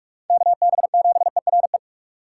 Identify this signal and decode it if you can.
Morse code, heard as a keyed tone, about 39 wpm, 700 Hz. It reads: KB7ERE